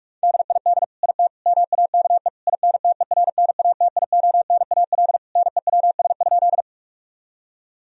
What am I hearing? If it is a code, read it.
Morse code, heard as a keyed tone, about 41 wpm, 700 Hz. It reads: DID IT MAKE INTERNATIONAL NEWS?
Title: Sentences from Top 1000 Words - Review Part 4 - 45wpm